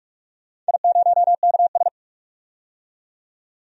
Morse code, heard as a keyed tone, about 42 wpm, 700 Hz. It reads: I0KS
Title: DXCC-100 Call Signs - 4 Characters - 45wpm